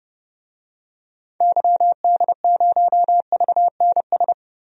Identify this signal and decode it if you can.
Morse code, heard as a keyed tone, about 30 wpm, 700 Hz. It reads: YD0VNS
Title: DXCC-100 Call Signs - 6 Characters - 30wpm